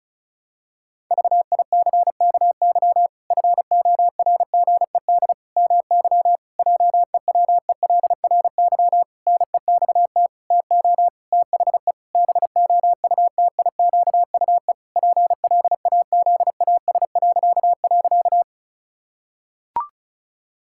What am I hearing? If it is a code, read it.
Morse code, heard as a keyed tone, about 36 wpm, 700 Hz. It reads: VICKY FORGED MY JEWELRY NEXT TO THE BOUTIQUE PLAZAS..
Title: Pangram Challenge v2 - 35wpm